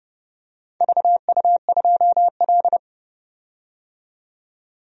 Morse code, heard as a keyed tone, about 30 wpm, 700 Hz. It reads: VU2L
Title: DXCC-100 Call Signs - 4 Characters - 30wpm